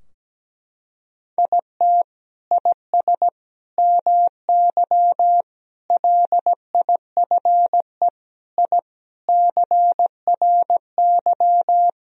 Morse code, heard as a keyed tone, about 17 wpm, 700 Hz. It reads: IT IS MY LIFE I CRY
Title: Sentences from Top 1000 Words - Review Part 2 - 17wpm